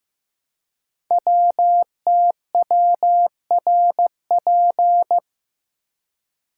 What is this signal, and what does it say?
Morse code, heard as a keyed tone, about 15 wpm, 700 Hz. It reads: WTWRP
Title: Code Groups - 5 Letters - 15wpm